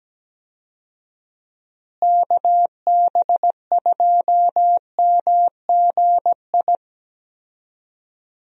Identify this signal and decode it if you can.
Morse code, heard as a keyed tone, about 17 wpm, 700 Hz. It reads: KB2MGI
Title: USA Call Signs - 6 Characters - Repeated 2x - 17wpm